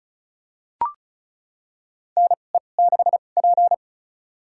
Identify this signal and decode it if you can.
Morse code, heard as a keyed tone, about 36 wpm, spaced as for 25 wpm, 700 Hz. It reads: NE6P
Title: USA Call Signs - 4 Characters - 2x Character Spacing - 35wpm